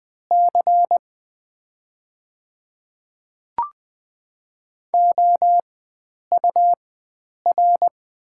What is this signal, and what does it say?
Morse code, heard as a keyed tone, about 20 wpm, spaced as for 9 wpm, 700 Hz. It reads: C OUR
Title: 4x Character Spacing - Top 300 Words - 20wpm